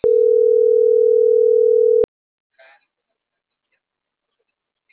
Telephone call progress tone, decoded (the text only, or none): ringback tone